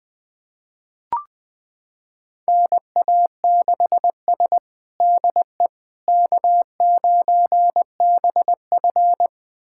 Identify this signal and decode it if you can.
Morse code, heard as a keyed tone, about 20 wpm, 700 Hz. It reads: NA6S DE K9BF
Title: USA Callsign DE Callsign - 4 Characters - 20wpm